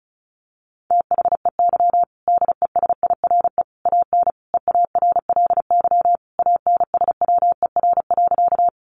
Morse code, heard as a keyed tone, about 34 wpm, 700 Hz. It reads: THEY DESIRE AN EARLY ANSWER.
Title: Sentences from Top 1100 Words 35wpm